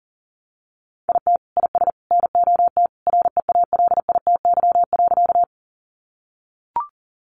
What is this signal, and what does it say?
Morse code, heard as a keyed tone, about 40 wpm, 700 Hz. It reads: IT IS NOT REALITY.
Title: Sentences from Top 1200 Words 40wpm